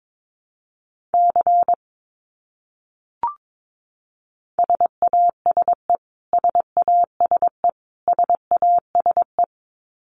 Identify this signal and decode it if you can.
Morse code, heard as a keyed tone, about 22 wpm, 700 Hz. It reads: C SASE SASE SASE
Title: Abbreviations Repeated 3x - 22wpm